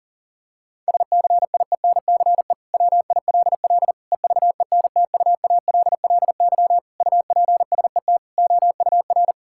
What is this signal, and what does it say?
Morse code, heard as a keyed tone, about 40 wpm, 700 Hz. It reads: SCIENCE WILL EVENTUALLY UPSET OUR
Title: Sentences from Top 1000 Words - Review Part 6 - 40wpm